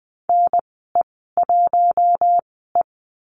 Morse code, heard as a keyed tone, about 20 wpm, spaced as for 14 wpm, 700 Hz. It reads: NE1E
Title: 069 Lesson 12g - Letter d - Call Signs 20wpm